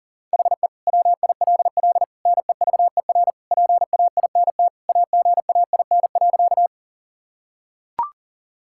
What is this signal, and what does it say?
Morse code, heard as a keyed tone, about 40 wpm, 700 Hz. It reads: HE WILL NEVER PAINT AGAIN.
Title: Sentences from Top 1000 Words - Review Part 6 - 40wpm